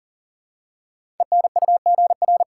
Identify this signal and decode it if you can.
Morse code, heard as a keyed tone, about 43 wpm, 700 Hz. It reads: ENUGR
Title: Code Groups - 5 Letters - 40wpm